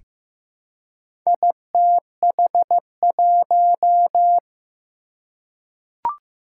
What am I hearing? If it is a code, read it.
Morse code, heard as a keyed tone, about 15 wpm, 700 Hz. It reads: ITH1
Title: Four Challenging Letter-Number Combinations 15wpm